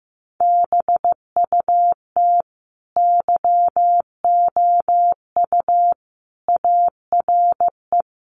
Morse code, heard as a keyed tone, about 15 wpm, 700 Hz. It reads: BUT YOU ARE
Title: Sentences from Top 1300 Words 15wpm